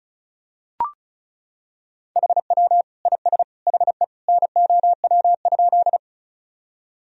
Morse code, heard as a keyed tone, about 36 wpm, 700 Hz. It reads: HW IS HE NOW?
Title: Sentences from Top 100 Words and Abbreviations 35wpm